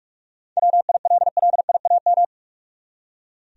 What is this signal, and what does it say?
Morse code, heard as a keyed tone, about 46 wpm, 700 Hz. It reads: WILLIAM
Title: USA Common Names - 45wpm